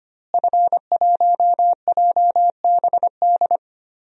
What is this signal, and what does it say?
Morse code, heard as a keyed tone, about 25 wpm, 700 Hz. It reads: F1JBD
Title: DXCC-100 Call Signs - 5 Characters - Repeated 3x - 25wpm